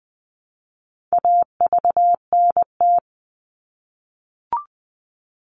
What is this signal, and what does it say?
Morse code, heard as a keyed tone, about 20 wpm, 700 Hz. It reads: AVNT
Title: All Cut Numbers - 4-Digit 20wpm